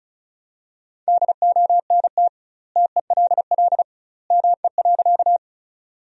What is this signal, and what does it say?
Morse code, heard as a keyed tone, about 34 wpm, 700 Hz. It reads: DONT TELL ME.
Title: 2x Word-Spacing - Sentences from Top 300 Words - 35wpm